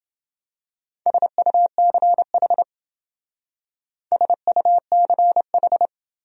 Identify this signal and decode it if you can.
Morse code, heard as a keyed tone, about 28 wpm, 700 Hz. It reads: SUCH SUCH